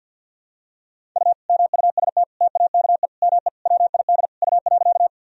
Morse code, heard as a keyed tone, about 53 wpm, 700 Hz. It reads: U MUST TAKE ME WID U.
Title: Sentences from Top 200 Words and Abbreviations 50wpm